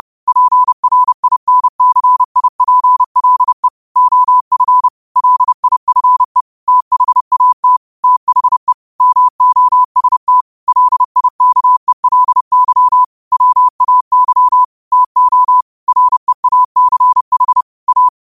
Morse code, heard as a keyed tone, about 30 wpm, 1000 Hz. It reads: PRINCIPLE OF LIFE THAT THE MOST LIKELY WAY TO REACH A